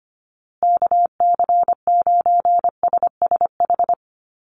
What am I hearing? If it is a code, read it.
Morse code, heard as a keyed tone, about 25 wpm, 700 Hz. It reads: KC9SSH
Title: USA Call Signs Repeated 2x 25wpm